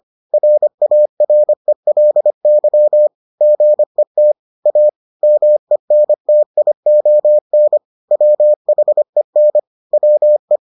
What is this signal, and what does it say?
Morse code, heard as a keyed tone, about 25 wpm, 600 Hz. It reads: RARELY GET A MENTION WHEN WE